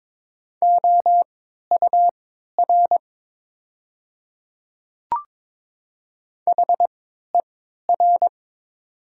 Morse code, heard as a keyed tone, about 22 wpm, spaced as for 13 wpm, 700 Hz. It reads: OUR HER
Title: 3x Character Spacing - Top 500 Words - 22wpm